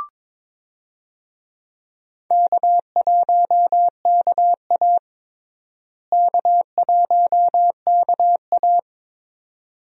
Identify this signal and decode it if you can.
Morse code, heard as a keyed tone, about 22 wpm, 700 Hz. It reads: K1KA K1KA
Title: USA Call Signs - 4 Characters - Repeated 2x - 22wpm